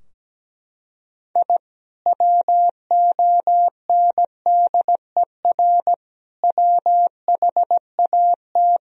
Morse code, heard as a keyed tone, about 17 wpm, 700 Hz. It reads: I WONDER WHAT
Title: Sentences from Top 1000 Words - Review Part 3 - 17wpm